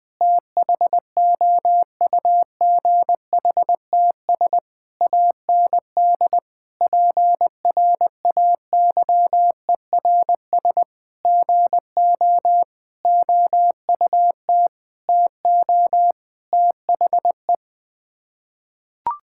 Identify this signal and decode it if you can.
Morse code, heard as a keyed tone, about 20 wpm, 700 Hz. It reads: THOUGHTS AND PRAYERS GO OUT TO THE
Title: Sets of 8 Words - 20wpm